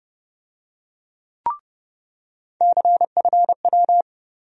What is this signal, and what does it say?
Morse code, heard as a keyed tone, about 31 wpm, 700 Hz. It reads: CFW
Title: USA Call Sign Postfixes - 3 Letters 30wpm